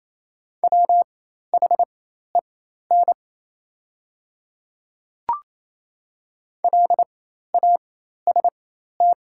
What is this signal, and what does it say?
Morse code, heard as a keyed tone, about 27 wpm, spaced as for 13 wpm, 700 Hz. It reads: WHEN LAST
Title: 4x Character Spacing - Top 500 Words - 28wpm